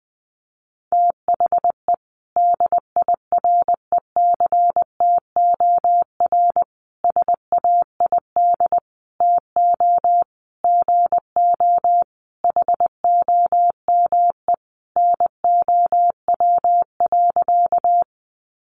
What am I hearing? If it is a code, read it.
Morse code, heard as a keyed tone, about 20 wpm, 700 Hz. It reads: THE DIRECTOR SAID TO GO HOME NOW.
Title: Sentences from Top 1300 Words 20wpm